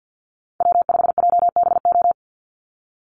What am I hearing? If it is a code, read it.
Morse code, heard as a keyed tone, about 50 wpm, 700 Hz. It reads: W5JBO
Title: Call Signs Repeated 3x - 50wpm